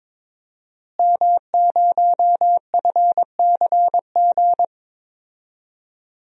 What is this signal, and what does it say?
Morse code, heard as a keyed tone, about 22 wpm, 700 Hz. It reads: M0FCG
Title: DXCC-100 Call Signs - 5 Characters - Repeated 2x - 22wpm